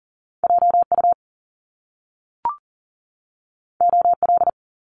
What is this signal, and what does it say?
Morse code, heard as a keyed tone, about 39 wpm, 700 Hz. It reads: JU OL
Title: Top Two Letter Combinations in Words 40wpm